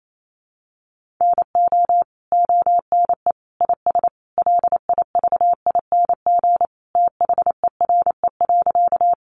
Morse code, heard as a keyed tone, about 28 wpm, 700 Hz. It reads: NO ONE IS LIVING THERE.
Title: Sentences from Top 1100 Words 28wpm